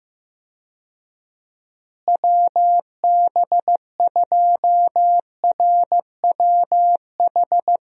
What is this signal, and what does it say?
Morse code, heard as a keyed tone, about 15 wpm, 700 Hz. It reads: WB2RWH
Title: USA Call Signs - 6 Characters - Repeated 3x - 15wpm